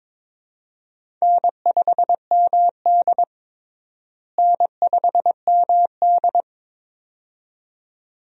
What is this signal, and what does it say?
Morse code, heard as a keyed tone, about 22 wpm, 700 Hz. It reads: N5MD N5MD
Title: USA Call Signs - 4 Characters - Repeated 2x - 22wpm